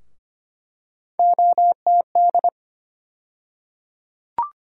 Morse code, heard as a keyed tone, about 25 wpm, 700 Hz. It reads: OTD
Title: USA Call Sign Postfixes - 3 Letters 25wpm